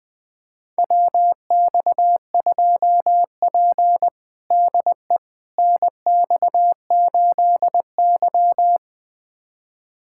Morse code, heard as a keyed tone, about 20 wpm, 700 Hz. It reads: WX2P DE NX8Y
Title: USA Callsign DE Callsign - 4 Characters - 20wpm